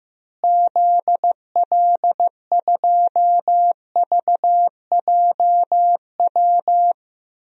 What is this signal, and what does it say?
Morse code, heard as a keyed tone, about 15 wpm, 700 Hz. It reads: ZL2VJW
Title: DXCC-100 Call Signs - 6 Characters - Repeated 3x - 15wpm